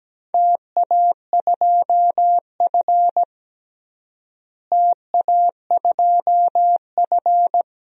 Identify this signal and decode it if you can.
Morse code, heard as a keyed tone, about 17 wpm, 700 Hz. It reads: TA2F TA2F
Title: DXCC-100 Call Signs - 4 Characters - Repeated 2x - 17wpm